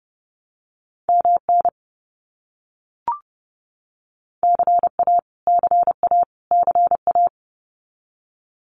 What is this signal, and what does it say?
Morse code, heard as a keyed tone, about 30 wpm, 700 Hz. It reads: MN CA CA CA
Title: US States Repeated 3x 30wpm